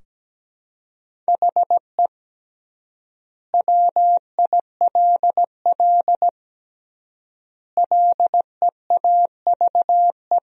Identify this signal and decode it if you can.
Morse code, heard as a keyed tone, about 17 wpm, 700 Hz. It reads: HE WILL LEAVE
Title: 3x Word-Spacing - Sentences from Top 400 Words - 17wpm